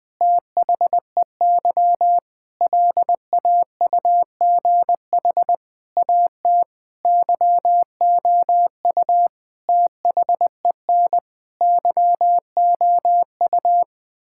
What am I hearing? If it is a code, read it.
Morse code, heard as a keyed tone, about 20 wpm, 700 Hz. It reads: THEY LAUGH AT YOU THEN YOU
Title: Sentences with 8-Word Phrases - Spoken Beginning - 20wpm